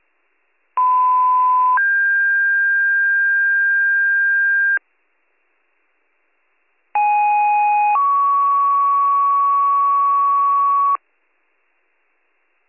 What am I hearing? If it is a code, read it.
Two-tone page, heard as a one-second tone, then a three-second tone, about 1010 then 1600 Hz, about 830 then 1120 Hz. It is heard from 0.8 to 11.0 s.